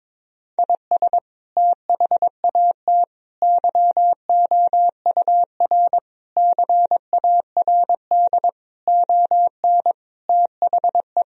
Morse code, heard as a keyed tone, about 21 wpm, 700 Hz. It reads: IS THAT YOUR CARD ON THE
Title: Sentences from Top 1000 Words - Review Part 4 - 22wpm